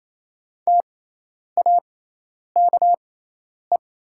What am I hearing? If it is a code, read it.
Morse code, heard as a keyed tone, about 28 wpm, spaced as for 10 wpm, 700 Hz. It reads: TAKE
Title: 6x Character Spacing - Top 500 Words - 28wpm